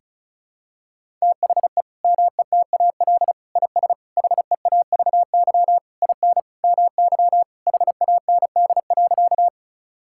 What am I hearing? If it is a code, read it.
Morse code, heard as a keyed tone, about 36 wpm, 700 Hz. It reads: THE METAL IS HEAVY IN MY HAND.
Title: Sentences from Top 1000 Words - Review Part 6 - 35wpm